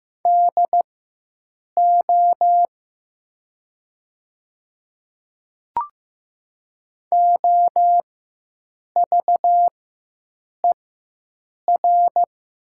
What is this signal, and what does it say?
Morse code, heard as a keyed tone, about 15 wpm, spaced as for 7 wpm, 700 Hz. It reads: DO OVER
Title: 4x Character Spacing - Top 100 Words - 15wpm